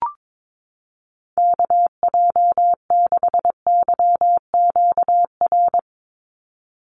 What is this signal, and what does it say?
Morse code, heard as a keyed tone, about 22 wpm, 700 Hz. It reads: KJ6YQR
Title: USA Call Signs 22wpm